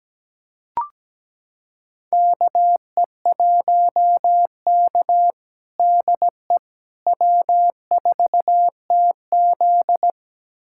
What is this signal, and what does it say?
Morse code, heard as a keyed tone, about 17 wpm, 700 Hz. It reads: KE1K DE W4TZ
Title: USA Callsign DE Callsign - 4 Characters - 17wpm